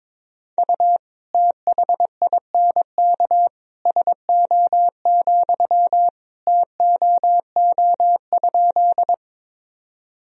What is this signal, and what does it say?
Morse code, heard as a keyed tone, about 22 wpm, 700 Hz. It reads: U THINK SO, TOO?
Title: Sentences from Top 200 Words and Abbreviations 22wpm